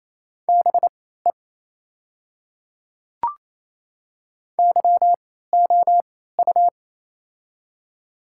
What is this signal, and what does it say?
Morse code, heard as a keyed tone, about 28 wpm, spaced as for 16 wpm, 700 Hz. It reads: BE YOU
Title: 3x Character Spacing - Top 200 Words - 28wpm